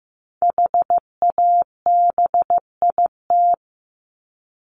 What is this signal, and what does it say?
Morse code, heard as a keyed tone, about 15 wpm, 700 Hz. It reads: HABIT
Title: Top 1751-2000 Words 15wpm